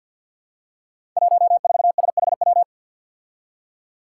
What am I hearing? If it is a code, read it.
Morse code, heard as a keyed tone, about 52 wpm, 700 Hz. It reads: 14SHW
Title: Five Challenging Letter-Number Combinations 50wpm